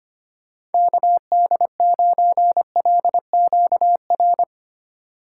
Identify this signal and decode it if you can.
Morse code, heard as a keyed tone, about 31 wpm, 700 Hz. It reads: KD9LQR
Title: Speed Racing - USA 2x3 Call Signs 25wpm